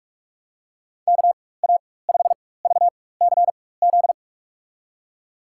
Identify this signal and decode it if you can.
Morse code, heard as a keyed tone, about 44 wpm, spaced as for 21 wpm, 700 Hz. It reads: KA5VCZ
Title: USA Call Signs - 6 Characters - 4x Character Spacing - 45wpm